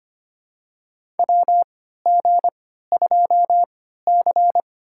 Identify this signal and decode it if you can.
Morse code, heard as a keyed tone, about 25 wpm, spaced as for 14 wpm, 700 Hz. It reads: WG2C
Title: USA Call Signs - 4 Characters - 3x Character Spacing - 25wpm